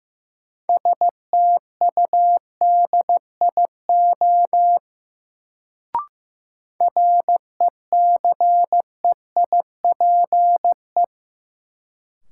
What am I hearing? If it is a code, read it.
Morse code, heard as a keyed tone, about 15 wpm, 700 Hz. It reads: STUDIO RECEIPE
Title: Sentences from Top 1200 Words - New Word Drill Alt - 15wpm